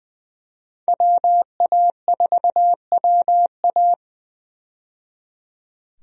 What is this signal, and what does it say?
Morse code, heard as a keyed tone, about 20 wpm, 700 Hz. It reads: WA4WA
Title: USA Call Signs - 5 Characters - 20wpm